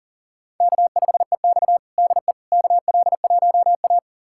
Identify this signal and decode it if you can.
Morse code, heard as a keyed tone, about 37 wpm, 700 Hz. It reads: K5EX DE KL1A
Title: USA Callsign DE Callsign - 4 Characters - 40wpm